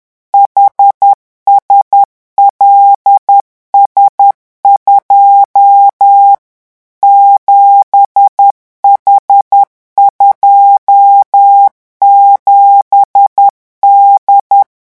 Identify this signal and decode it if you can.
Morse code, heard as a keyed tone, about 10 wpm, 800 Hz. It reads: HSLS2 7H27D
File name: morse mp3 / revision13-vitesse 10 mots